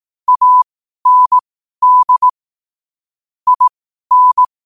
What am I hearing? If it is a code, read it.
Morse code, heard as a keyed tone, about 18 wpm, spaced as for 12 wpm, 1000 Hz. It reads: AND IN
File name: qotdfiles / 12wpm